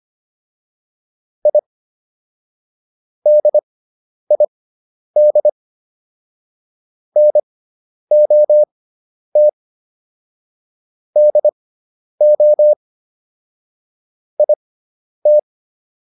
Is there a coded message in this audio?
Yes: Morse code from 1.5 to 15.4 s, heard as a keyed tone, about 25 wpm, spaced as for 10 wpm, 600 Hz. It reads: I DID NOT DO IT